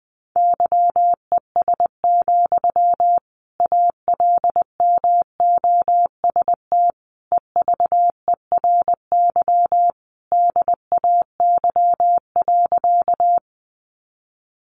Morse code, heard as a keyed tone, about 20 wpm, 700 Hz. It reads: YES, ALMOST EVERY DAY.
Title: Sentences from Top 700 Words 20wpm